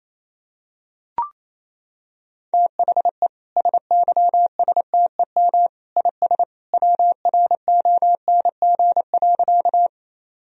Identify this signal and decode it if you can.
Morse code, heard as a keyed tone, about 28 wpm, 700 Hz. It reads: THE SYSTEM IS WRONG.
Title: Sentences from Top 1000 Words - Review Part 3 - 28wpm